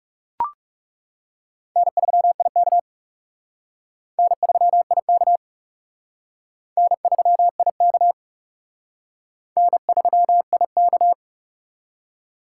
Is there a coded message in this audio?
Yes: Morse code from 1.8 to 11.1 s, heard as a keyed tone, about 40 wpm, 700 Hz. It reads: N3IK N3IK N3IK N3IK